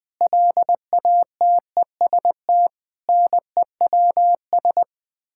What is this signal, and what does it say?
Morse code, heard as a keyed tone, about 20 wpm, 700 Hz. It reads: LATEST NEWS
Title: Sets of 9 Words - 20wpm